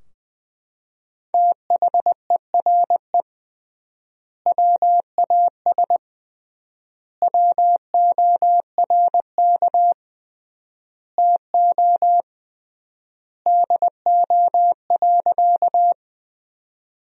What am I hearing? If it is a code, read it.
Morse code, heard as a keyed tone, about 20 wpm, 700 Hz. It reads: THERE WAS WORK TO DO.
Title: 3x Word-Spacing - Sentences from Top 200 Words - 20wpm